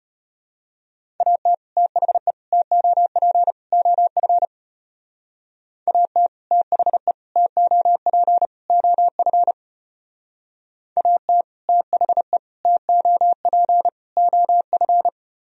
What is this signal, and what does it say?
Morse code, heard as a keyed tone, about 32 wpm, 700 Hz. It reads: AT THE TOP OF AT THE TOP OF AT THE TOP OF